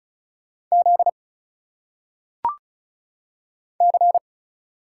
Morse code, heard as a keyed tone, about 35 wpm, 700 Hz. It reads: Z C
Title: Single Letters - Phonetic Pronunciation 35wpm